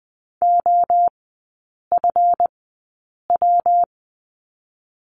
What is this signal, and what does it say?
Morse code, heard as a keyed tone, about 20 wpm, spaced as for 8 wpm, 690 Hz. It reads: OFW
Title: 159 Lesson 26e - Letter g - Three Characters 20wpm